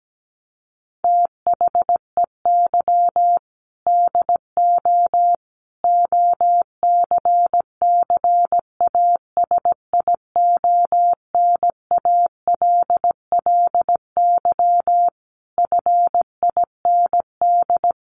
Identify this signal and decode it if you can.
Morse code, heard as a keyed tone, about 17 wpm, 700 Hz. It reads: THEY DO OCCASIONALLY FIND
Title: Sentences from Top 1300 Words 17wpm